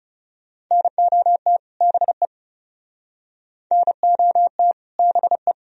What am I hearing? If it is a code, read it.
Morse code, heard as a keyed tone, about 37 wpm, 700 Hz. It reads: NOT BE NOT BE
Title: Speed-Racing - Sets of 2 Words 30wpm